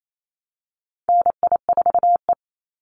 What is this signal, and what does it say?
Morse code, heard as a keyed tone, about 28 wpm, 700 Hz. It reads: NI4E
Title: Call Signs - 28wpm